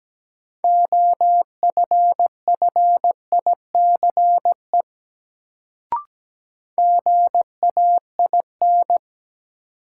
Morse code, heard as a keyed tone, about 17 wpm, 700 Hz. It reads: OFFICE GAIN
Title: Sentences from Top 1000 Words - New Word Drill - 17wpm